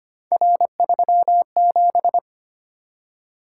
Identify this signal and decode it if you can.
Morse code, heard as a keyed tone, about 25 wpm, 700 Hz. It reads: R37
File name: Three Letter-Number Combinations - 25wpm